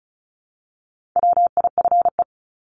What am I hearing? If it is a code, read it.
Morse code, heard as a keyed tone, about 34 wpm, 700 Hz. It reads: WIFE
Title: Top 1751-2000 Words 35wpm